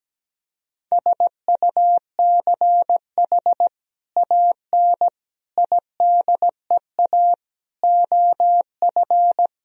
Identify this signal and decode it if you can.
Morse code, heard as a keyed tone, about 17 wpm, 700 Hz. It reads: SUCH AN IDEA OF
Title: Sentences from Top 1000 Words - Review Part 3 - 17wpm